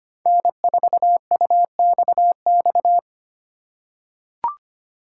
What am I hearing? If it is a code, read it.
Morse code, heard as a keyed tone, about 25 wpm, 700 Hz. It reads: N4UXX
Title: USA Call Signs - 5 Characters - 2x Character Spacing - 25wpm